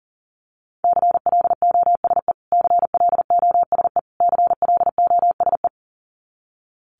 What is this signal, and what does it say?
Morse code, heard as a keyed tone, about 40 wpm, 700 Hz. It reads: CLOSE CLOSE CLOSE
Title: Top 500 Words Repeated 40wpm